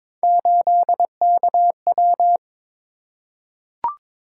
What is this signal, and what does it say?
Morse code, heard as a keyed tone, about 22 wpm, 700 Hz. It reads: 8KW
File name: practice / 22wpm / Three Letter-Number Combinations - 22wpm